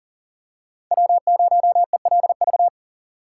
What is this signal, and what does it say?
Morse code, heard as a keyed tone, about 41 wpm, 690 Hz. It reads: W0ELV
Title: Code Groups - 5 Letter:Numbers - 40wpm